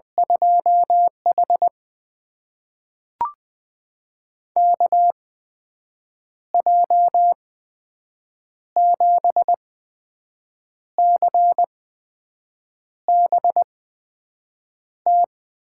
Morse code, heard as a keyed tone, about 20 wpm, spaced as for 6 wpm, 700 Hz. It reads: 2H KJ7CBT
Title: USA Call Signs - 8x Character Spacing - 20wpm